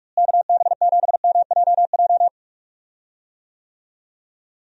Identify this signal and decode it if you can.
Morse code, heard as a keyed tone, about 44 wpm, 700 Hz. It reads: KB7MJJ
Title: USA Call Signs - 6 Characters - Repeated 3x - 45wpm